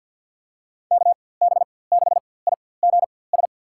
Morse code, heard as a keyed tone, about 47 wpm, spaced as for 23 wpm, 700 Hz. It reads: KB6IGS
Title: USA Call Signs - 6 Characters - 4x Character Spacing - 50wpm